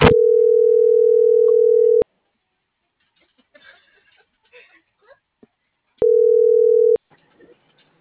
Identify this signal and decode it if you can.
Ringback tone (North America), 440 + 480 Hz.